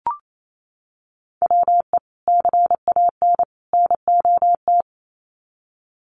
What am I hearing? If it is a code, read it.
Morse code, heard as a keyed tone, about 28 wpm, 700 Hz. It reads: WE CAN NOT
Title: Sets of 3 Words 28wpm